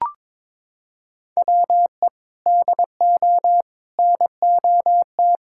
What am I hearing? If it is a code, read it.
Morse code, heard as a keyed tone, about 22 wpm, 700 Hz. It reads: WE DO NOT
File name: practice / 22wpm / Sentences with 4-Word Phrases - 22wpm